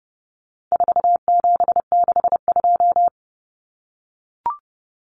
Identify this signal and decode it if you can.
Morse code, heard as a keyed tone, about 30 wpm, 700 Hz. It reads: 4762
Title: Numbers - 4-Digit 30wpm